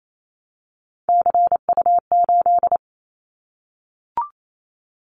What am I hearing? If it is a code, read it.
Morse code, heard as a keyed tone, about 28 wpm, 700 Hz. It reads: CU8
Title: Callsign Prefixes for Top 100 DXCC - 28wpm